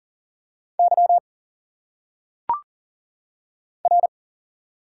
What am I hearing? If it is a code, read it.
Morse code, heard as a keyed tone, about 40 wpm, 700 Hz. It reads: Y R
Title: Single Letters - Phonetic Pronunciation 40wpm